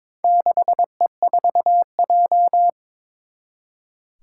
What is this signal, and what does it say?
Morse code, heard as a keyed tone, about 22 wpm, 700 Hz. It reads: 6E4J
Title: Four Challenging Letter-Number Combinations 22wpm